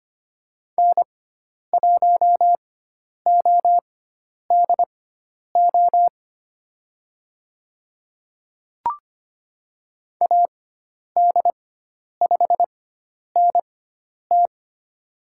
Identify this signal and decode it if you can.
Morse code, heard as a keyed tone, about 25 wpm, spaced as for 10 wpm, 700 Hz. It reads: N1ODO AD5NT
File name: Lesson 16g - Letter c - Call Signs 25wpm